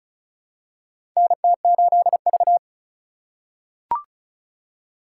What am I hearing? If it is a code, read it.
Morse code, heard as a keyed tone, about 36 wpm, 700 Hz. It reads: NT8V
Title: USA Call Signs - 4 Characters - 3x Character Spacing - 35wpm